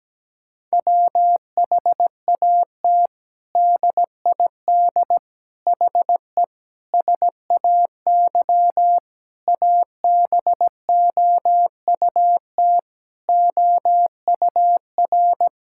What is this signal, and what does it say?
Morse code, heard as a keyed tone, about 17 wpm, 700 Hz. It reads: WHAT DID HE SAY ABOUT OUR
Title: Sentences from Top 1000 Words - Review Part 6 - 17wpm